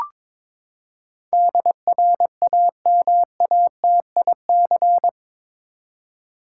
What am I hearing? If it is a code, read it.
Morse code, heard as a keyed tone, about 21 wpm, 700 Hz. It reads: DRAMATIC
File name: practice / 22wpm / Eight Letter Words - 22wpm